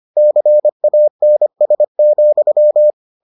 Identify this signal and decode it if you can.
Morse code, heard as a keyed tone, about 25 wpm, 600 Hz. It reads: CANS,